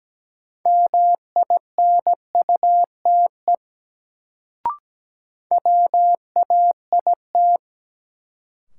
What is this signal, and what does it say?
Morse code, heard as a keyed tone, about 17 wpm, 700 Hz. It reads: MINUTE WAIT
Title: Sentences from Top 500 Words - New Word Drill - 17wpm